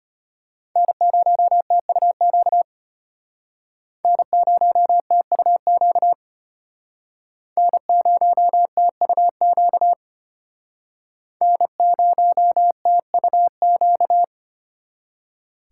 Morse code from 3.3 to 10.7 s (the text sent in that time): N0TUQ N0TUQ